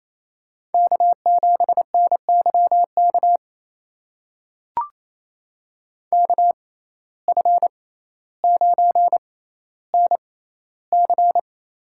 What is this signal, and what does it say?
Morse code, heard as a keyed tone, about 28 wpm, spaced as for 14 wpm, 700 Hz. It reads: K7NYK KF9NC
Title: USA Call Signs - 5 Characters - 6x Character Spacing - 28wpm